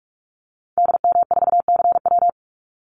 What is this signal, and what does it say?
Morse code, heard as a keyed tone, about 44 wpm, 700 Hz. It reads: DM4CW
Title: Call Signs - 45wpm